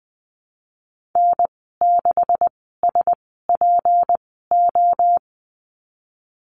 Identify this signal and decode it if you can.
Morse code, heard as a keyed tone, about 20 wpm, spaced as for 14 wpm, 700 Hz. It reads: N6SPO
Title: 161 Lesson 26g - Letter g - Call Signs 20wpm